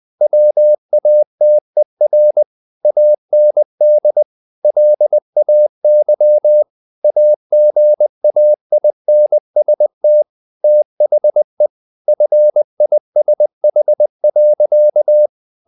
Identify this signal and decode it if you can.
Morse code, heard as a keyed tone, about 20 wpm, 600 Hz. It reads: WATER AND LAY AGAINST THE FISH.
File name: the old man and the sea part 2 20wpm